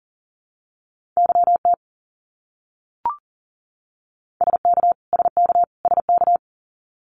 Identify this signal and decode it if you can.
Morse code, heard as a keyed tone, about 40 wpm, 700 Hz. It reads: YT SK SK SK
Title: Canadian Provinces and Territories - Repeated 3x - 40wpm